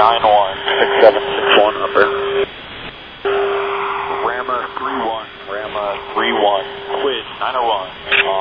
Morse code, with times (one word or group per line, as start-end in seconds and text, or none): none